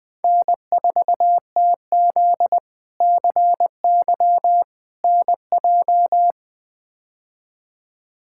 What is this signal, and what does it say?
Morse code, heard as a keyed tone, about 20 wpm, 700 Hz. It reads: N4TZ CY NJ
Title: K1USN SST QSO Practice - 20wpm